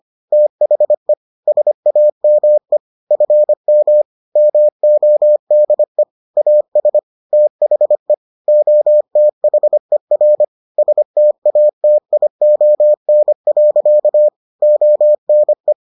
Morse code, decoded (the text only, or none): THE SAME FM MODE AS THE OTHER STATION. ONE